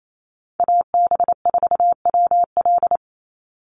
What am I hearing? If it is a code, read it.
Morse code, heard as a keyed tone, about 28 wpm, 700 Hz. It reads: AB4WL
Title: NA SOTA Chaser Callsigns 28wpm